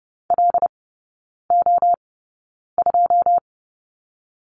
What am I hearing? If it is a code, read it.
Morse code, heard as a keyed tone, about 30 wpm, spaced as for 9 wpm, 700 Hz. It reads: LO2
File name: Lesson 17e - Letter u - Three Characters 30wpm